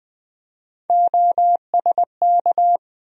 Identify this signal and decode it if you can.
Morse code, heard as a keyed tone, about 20 wpm, 700 Hz. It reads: OSK
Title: USA Call Sign Postfixes - 3 Letters 20wpm